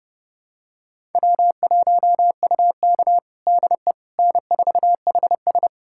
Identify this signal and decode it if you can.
Morse code, heard as a keyed tone, about 30 wpm, 700 Hz. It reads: W1UK DE N4HS
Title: USA Callsign DE Callsign - 4 Characters - 30wpm